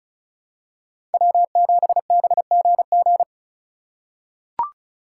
Morse code, heard as a keyed tone, about 34 wpm, 700 Hz. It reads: W7BGG
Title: USA Call Signs - 5 Characters - 4x Character Spacing - 35wpm